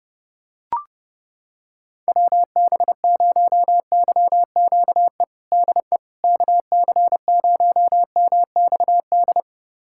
Morse code, heard as a keyed tone, about 30 wpm, 700 Hz. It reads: WB0YQE DE KC0MXD
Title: USA Callsign DE Callsign - 6 Characters - 30wpm